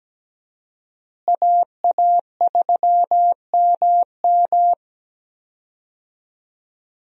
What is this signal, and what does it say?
Morse code, heard as a keyed tone, about 17 wpm, 700 Hz. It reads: AA3MM
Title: USA Call Signs - 5 Characters - Repeated 3x - 17wpm